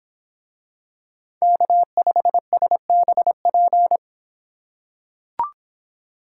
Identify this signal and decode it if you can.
Morse code, heard as a keyed tone, about 26 wpm, 700 Hz. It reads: K5SBP